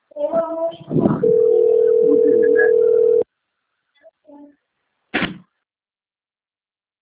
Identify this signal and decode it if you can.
Ringback tone (North America), 440 + 480 Hz.
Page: Centro Rio